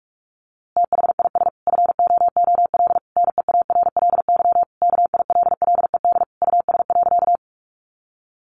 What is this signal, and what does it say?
Morse code, heard as a keyed tone, about 44 wpm, 700 Hz. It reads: THIS FOOL NEARLY KILLED US.
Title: A Sound Of Thunder 45wpm